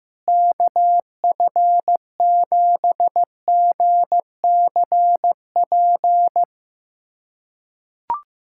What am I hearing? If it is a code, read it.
Morse code, heard as a keyed tone, about 16 wpm, 700 Hz. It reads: KF7GCP
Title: USA Callsign DE Callsign - 6 Characters - 15wpm